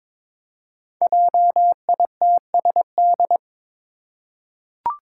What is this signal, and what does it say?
Morse code, heard as a keyed tone, about 22 wpm, 700 Hz. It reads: JITSD
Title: Five Challenging Letter-Number Combinations 22wpm